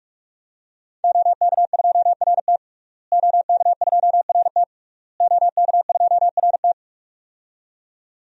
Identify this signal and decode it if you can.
Morse code, heard as a keyed tone, about 44 wpm, 700 Hz. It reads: OK2RT OK2RT OK2RT
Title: DXCC-100 Call Signs - 5 Characters - Repeated 3x - 45wpm